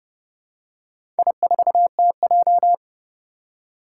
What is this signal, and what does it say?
Morse code, heard as a keyed tone, about 28 wpm, 700 Hz. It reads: I4TJ
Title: Four Challenging Letter-Number Combinations 30wpm